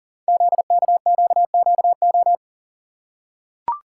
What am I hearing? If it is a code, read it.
Morse code, heard as a keyed tone, about 42 wpm, 700 Hz. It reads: ZKQQO
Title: Code Groups - 5 Letters - 40wpm